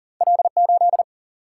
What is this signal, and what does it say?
Morse code, heard as a keyed tone, about 40 wpm, 700 Hz. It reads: L 8
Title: Single-Letter Single-Number 40wpm